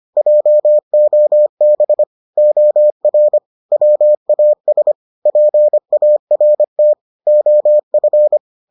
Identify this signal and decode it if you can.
Morse code, heard as a keyed tone, about 25 wpm, 600 Hz. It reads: JOB OR WAS PART OF